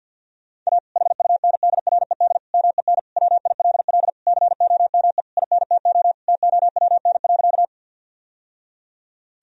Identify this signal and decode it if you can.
Morse code, heard as a keyed tone, about 50 wpm, 700 Hz. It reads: A HUNDRED MEN WILL COME INTO TOWN.
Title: Sentences from Top 1000 Words - Review Part 2 - 50wpm